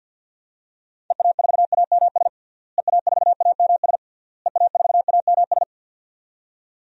Morse code, heard as a keyed tone, about 46 wpm, 700 Hz. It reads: EA4AMS EA4AMS EA4AMS
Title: DXCC-100 Call Signs - 6 Characters - Repeated 3x - 50wpm